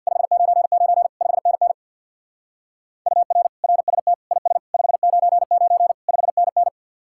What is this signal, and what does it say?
Morse code, heard as a keyed tone, about 59 wpm, 700 Hz. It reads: 599 5NN UR RST IS 599 5NN
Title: Speed-Racing Common QSO Phrases Encore - 50wpm